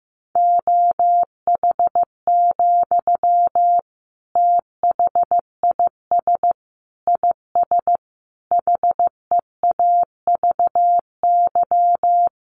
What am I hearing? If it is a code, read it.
Morse code, heard as a keyed tone, about 15 wpm, 700 Hz. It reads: OH, THIS IS HEAVY
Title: Sentences from Top 1100 Words 15wpm